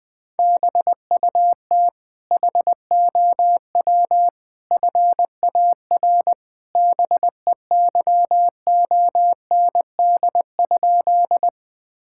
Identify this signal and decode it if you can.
Morse code, heard as a keyed tone, about 21 wpm, 700 Hz. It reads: BUT HOW FAR BEYOND?
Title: Sentences from Top 1000 Words - Review Part 6 - 20wpm